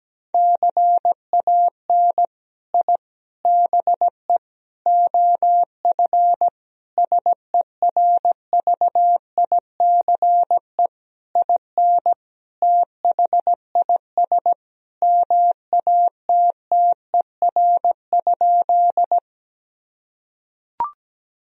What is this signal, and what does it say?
Morse code, heard as a keyed tone, about 17 wpm, 700 Hz. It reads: CAN I BE OF SERVICE IN THIS MATTER?
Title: Sentences from Top 1000 Words - Review Part 4 - 17wpm